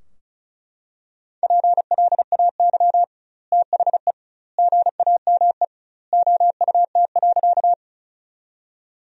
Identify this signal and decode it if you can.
Morse code, heard as a keyed tone, about 36 wpm, 700 Hz. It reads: PLAY THE GAME OUT.
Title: 2x Word-Spacing - Sentences from Top 500 Words - 35wpm